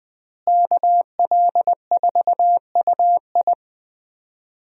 Morse code, heard as a keyed tone, about 20 wpm, 700 Hz. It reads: KL4UI
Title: USA Call Signs - 5 Characters - Repeated 3x - 20wpm